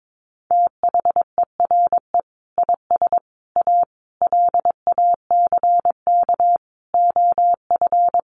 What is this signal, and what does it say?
Morse code, heard as a keyed tone, about 22 wpm, 700 Hz. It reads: THERE IS A LACK OF
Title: Sentences from Top 1100 Words 22wpm